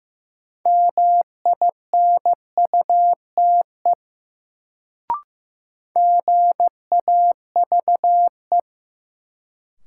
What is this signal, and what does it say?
Morse code, heard as a keyed tone, about 15 wpm, 700 Hz. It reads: MINUTE GAVE
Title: Sentences from Top 500 Words - New Word Drill - 15wpm